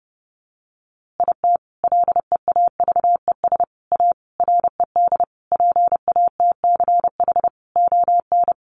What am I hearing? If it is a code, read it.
Morse code, heard as a keyed tone, about 30 wpm, 700 Hz. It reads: IT LEAVES A RED PATCH ON
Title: A Sound Of Thunder 30wpm